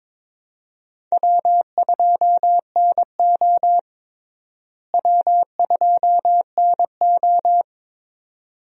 Morse code, heard as a keyed tone, about 22 wpm, 700 Hz. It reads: W2NO W2NO
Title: USA Call Signs - 4 Characters - Repeated 2x - 22wpm